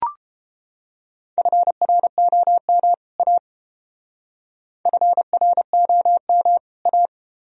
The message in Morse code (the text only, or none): FROM A FROM A